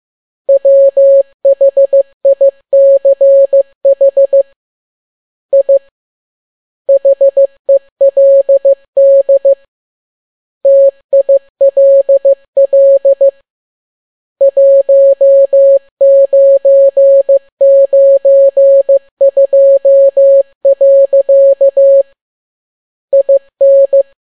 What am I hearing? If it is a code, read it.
Morse code, heard as a keyed tone, about 15 wpm, 550 Hz. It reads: WHICH I HELD TILL 1992. IN